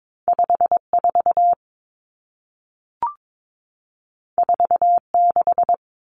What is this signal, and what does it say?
Morse code, heard as a keyed tone, about 22 wpm, 700 Hz. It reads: 54 46
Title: Numbers - 2-Digit 22wpm